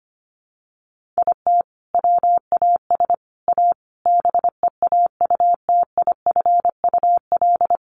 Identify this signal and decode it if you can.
Morse code, heard as a keyed tone, about 25 wpm, 690 Hz. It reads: IT WAS A BEAUTIFUL
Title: Sentences from Top 1100 Words 25wpm